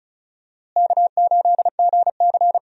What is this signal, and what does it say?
Morse code, heard as a keyed tone, about 36 wpm, 700 Hz. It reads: K8GC
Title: USA Call Signs - 4 Characters - Repeated 3x - 35wpm